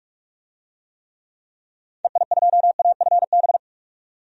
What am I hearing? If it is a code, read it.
Morse code, heard as a keyed tone, about 46 wpm, 700 Hz. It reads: EI2AFB
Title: DXCC-100 Call Signs - 6 Characters - 45wpm